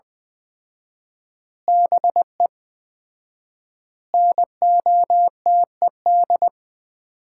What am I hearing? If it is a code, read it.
Morse code, heard as a keyed tone, about 20 wpm, 700 Hz. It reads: BE NOTED
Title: 4x Word-Spacing - Sets of 5 Words - 20wpm